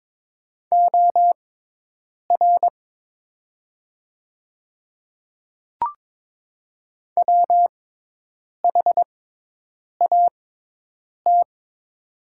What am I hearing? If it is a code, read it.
Morse code, heard as a keyed tone, about 22 wpm, spaced as for 8 wpm, 700 Hz. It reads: OR WHAT
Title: 6x Character Spacing - Top 300 Words - 22wpm